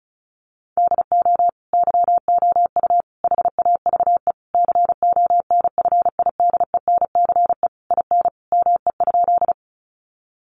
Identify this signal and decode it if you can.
Morse code, heard as a keyed tone, about 34 wpm, 700 Hz. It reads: DO YOU HAVE CONFIDENCE IN ME?
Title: Sentences from Top 1500 Words 35wpm